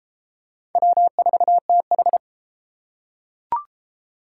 Morse code, heard as a keyed tone, about 33 wpm, 700 Hz. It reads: W4TH